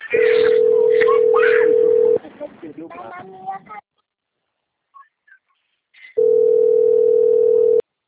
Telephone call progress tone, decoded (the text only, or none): ringback tone